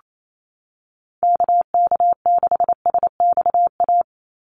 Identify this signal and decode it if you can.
Morse code, heard as a keyed tone, about 28 wpm, 700 Hz. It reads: KK6SXA
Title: USA Call Signs 28wpm